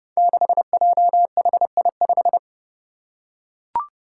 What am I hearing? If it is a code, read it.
Morse code, heard as a keyed tone, about 30 wpm, 700 Hz. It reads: 6JHI5
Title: Five Challenging Letter-Number Combinations 30wpm